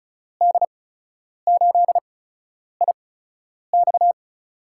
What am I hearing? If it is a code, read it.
Morse code, heard as a keyed tone, about 36 wpm, spaced as for 10 wpm, 700 Hz. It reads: D8IX
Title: USA Call Signs - 5 Characters - 8x Character Spacing - 35wpm